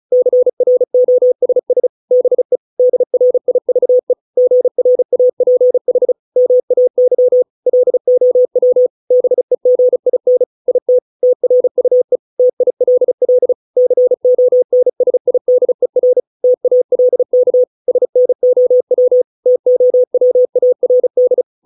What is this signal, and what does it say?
Morse code, heard as a keyed tone, about 35 wpm, 500 Hz. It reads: CROSS BE DRIVE GRAPH MAY LOW BEGIN IT TRUE TILL CONSIDER TALK SNOW TOWARD